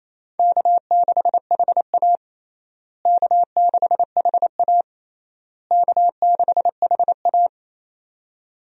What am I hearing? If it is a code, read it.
Morse code, heard as a keyed tone, about 28 wpm, 700 Hz. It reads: K6HA K6HA K6HA
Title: USA Call Signs - 4 Characters - Repeated 3x - 28wpm